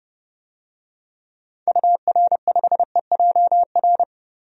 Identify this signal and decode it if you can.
Morse code, heard as a keyed tone, about 30 wpm, 700 Hz. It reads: UR5EJR
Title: DXCC-100 Call Signs - 6 Characters - 30wpm